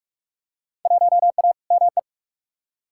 Morse code, heard as a keyed tone, about 45 wpm, 700 Hz. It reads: 1A ME
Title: Field Day Exchanges Encore - 45wpm